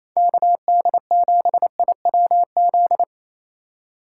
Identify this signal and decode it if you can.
Morse code, heard as a keyed tone, about 28 wpm, 700 Hz. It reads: KD7IWZ
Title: USA Call Signs - 6 Characters - 2x Character Spacing - 28wpm